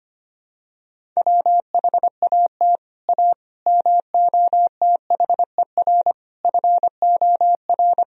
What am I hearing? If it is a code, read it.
Morse code, heard as a keyed tone, about 25 wpm, 700 Hz. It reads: WHAT A MOTHER FOR
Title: Sentences from Top 1000 Words - Review Part 1 - 25wpm